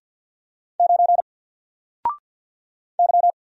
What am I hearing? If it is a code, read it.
Morse code, heard as a keyed tone, about 50 wpm, 700 Hz. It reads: 9 Y
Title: Single Letter-Number 50wpm